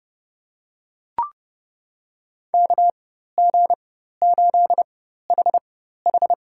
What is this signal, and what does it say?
Morse code, heard as a keyed tone, about 30 wpm, spaced as for 14 wpm, 700 Hz. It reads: KG8HH
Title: USA Call Signs - 5 Characters - 4x Character Spacing - 30wpm